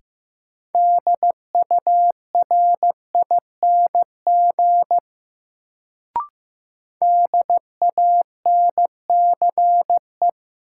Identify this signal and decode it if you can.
Morse code, heard as a keyed tone, about 15 wpm, 700 Hz. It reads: DURING DANCE
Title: Sentences from Top 700 Words - New Word Drill - 15wpm